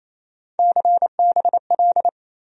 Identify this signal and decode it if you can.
Morse code, heard as a keyed tone, about 28 wpm, 700 Hz. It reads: CBL
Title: USA Call Sign Postfixes - 3 Letters 28wpm